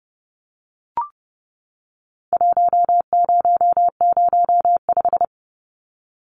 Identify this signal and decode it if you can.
Morse code, heard as a keyed tone, about 30 wpm, 700 Hz. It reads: 1005
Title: Numbers - Contest Style - 4-Digit 30wpm